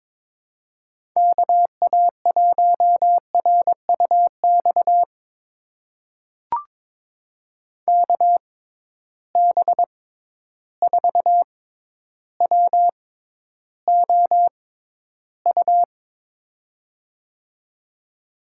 Morse code, heard as a keyed tone, about 23 wpm, spaced as for 11 wpm, 700 Hz. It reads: KA1RUX KB4WOU
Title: USA Call Signs - 6 Characters - 6x Character Spacing - 22wpm